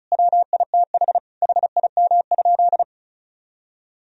Morse code, heard as a keyed tone, about 34 wpm, 700 Hz. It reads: WITH HIM?
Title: Sentences from Top 1000 Words - Review Part 1 - 35wpm